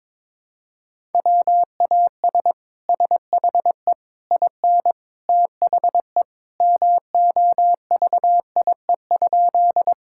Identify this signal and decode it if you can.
Morse code, heard as a keyed tone, about 22 wpm, 700 Hz. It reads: WAS SHE IN THE MOVIE?
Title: Sentences from Top 1000 Words - Review Part 5 - 22wpm